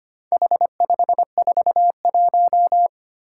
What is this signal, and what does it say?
Morse code, heard as a keyed tone, about 25 wpm, 700 Hz. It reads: H541
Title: Four Challenging Letter-Number Combinations 25wpm